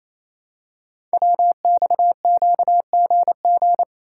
Morse code, heard as a keyed tone, about 29 wpm, 700 Hz. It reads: WXQGG
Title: Code Groups - 5 Letters - 28wpm